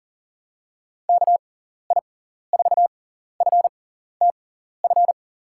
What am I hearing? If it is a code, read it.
Morse code, heard as a keyed tone, about 40 wpm, spaced as for 14 wpm, 700 Hz. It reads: KI4FTF
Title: USA Call Signs - 6 Characters - 6x Character Spacing - 40wpm